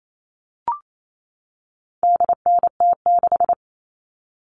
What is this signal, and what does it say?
Morse code, heard as a keyed tone, about 28 wpm, 700 Hz. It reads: DNT6
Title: All Cut Numbers - 4-Digit 28wpm